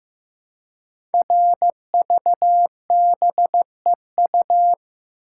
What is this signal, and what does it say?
Morse code, heard as a keyed tone, about 15 wpm, 700 Hz. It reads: RVBEU
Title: Code Groups - 5 Letters - 15wpm